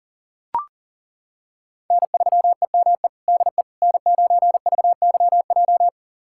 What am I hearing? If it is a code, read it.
Morse code, heard as a keyed tone, about 40 wpm, 700 Hz. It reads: N3EME DE N9VYJ
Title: USA Callsign DE Callsign - 5 Characters - 40wpm